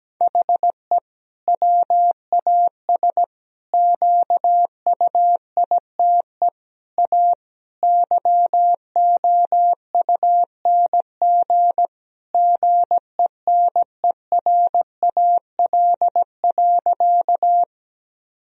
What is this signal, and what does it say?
Morse code, heard as a keyed tone, about 17 wpm, 700 Hz. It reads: HE WAS QUITE A YOUNG GENERAL.
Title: Sentences from Top 1000 Words - Review Part 3 - 17wpm